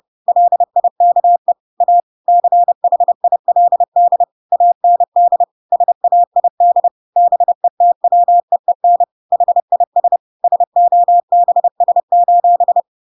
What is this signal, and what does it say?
Morse code, heard as a keyed tone, about 30 wpm, 700 Hz. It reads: LIKE A CHILD AND SAID BETWEEN HIS SOBS:
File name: Pinocchio / Pinocchio ch3 30 30 ik1hge